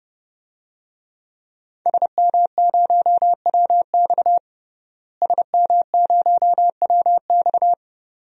Morse code, heard as a keyed tone, about 30 wpm, 700 Hz. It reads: SM0WX SM0WX
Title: DXCC-100 Call Signs - 5 Characters - Repeated 2x - 30wpm